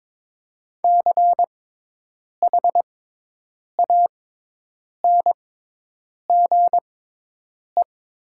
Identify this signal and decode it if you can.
Morse code, heard as a keyed tone, about 22 wpm, spaced as for 8 wpm, 700 Hz. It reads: CHANGE
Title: 6x Character Spacing - Top 200 Words - 22wpm